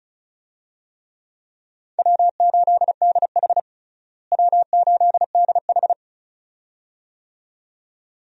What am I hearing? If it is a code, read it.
Morse code, heard as a keyed tone, about 36 wpm, 700 Hz. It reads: W8DH W8DH
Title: USA Call Signs - 4 Characters - Repeated 2x - 35wpm